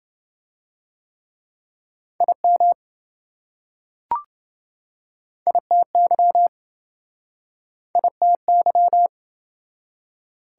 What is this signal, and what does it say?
Morse code, heard as a keyed tone, about 27 wpm, 700 Hz. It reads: IM ITY ITY